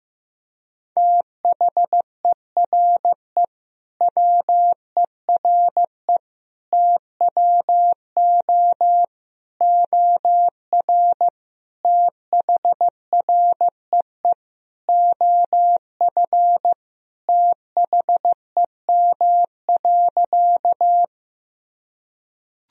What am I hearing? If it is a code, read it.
Morse code, heard as a keyed tone, about 16 wpm, 700 Hz. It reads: THERE WERE TWO OR THREE OF THEM.
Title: Sentences from Top 1000 Words - Review Part 1 - 15wpm